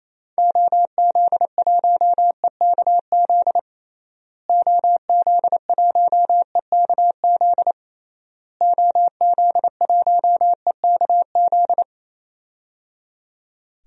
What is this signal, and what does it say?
Morse code, heard as a keyed tone, about 28 wpm, 700 Hz. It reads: OZ1EKZ OZ1EKZ OZ1EKZ
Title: DXCC-100 Call Signs - 6 Characters - Repeated 3x - 28wpm